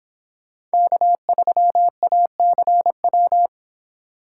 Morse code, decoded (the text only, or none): K3ACW